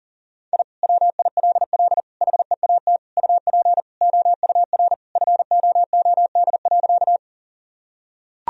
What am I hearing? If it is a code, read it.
Morse code, heard as a keyed tone, about 37 wpm, 700 Hz. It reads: I WILL HEAT UP OUR FOOD.
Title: Sentences from Top 1000 Words - Review Part 3 - 40wpm